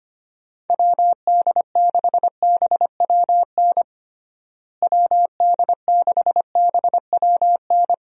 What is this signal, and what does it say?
Morse code, heard as a keyed tone, about 25 wpm, 700 Hz. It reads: WD6BWN WD6BWN
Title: USA Call Signs - 6 Characters - Repeated 2x - 25wpm